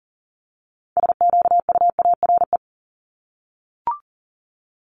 Morse code, heard as a keyed tone, about 39 wpm, 700 Hz. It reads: SQUARE
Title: Multi-letter Phonemes 40wpm